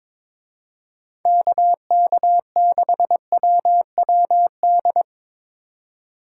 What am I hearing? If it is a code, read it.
Morse code, heard as a keyed tone, about 24 wpm, 700 Hz. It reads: KK6WWD
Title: Speed Racing - USA 2x3 Call Signs 22wpm